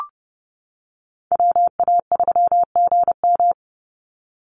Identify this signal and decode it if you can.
Morse code, heard as a keyed tone, about 28 wpm, 700 Hz. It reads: WA3GM
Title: Call Signs - 30wpm